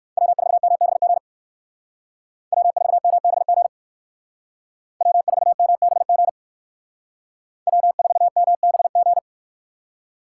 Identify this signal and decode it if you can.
Morse code, heard as a keyed tone, about 59 wpm, 700 Hz. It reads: W4MBG W4MBG W4MBG W4MBG